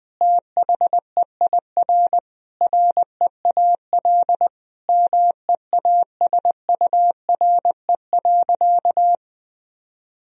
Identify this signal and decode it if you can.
Morse code, heard as a keyed tone, about 20 wpm, 700 Hz. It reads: THEIR REAL MEASURE.
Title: Sentences from Top 1000 Words - Review Part 1 - 20wpm